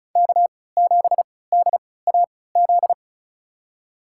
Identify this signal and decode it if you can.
Morse code, heard as a keyed tone, about 34 wpm, spaced as for 20 wpm, 700 Hz. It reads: K7DAZ
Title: USA Call Signs - 5 Characters - 3x Character Spacing - 35wpm